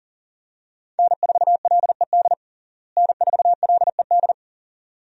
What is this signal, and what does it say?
Morse code, heard as a keyed tone, about 40 wpm, 700 Hz. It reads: N4LED N4LED
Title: USA Call Signs - 5 Characters - Repeated 2x - 40wpm